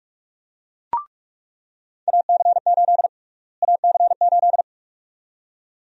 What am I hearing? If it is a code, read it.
Morse code, heard as a keyed tone, about 46 wpm, 700 Hz. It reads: AC8 AC8
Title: USA Call Sign Prefixes - Repeated 2x - 45wpm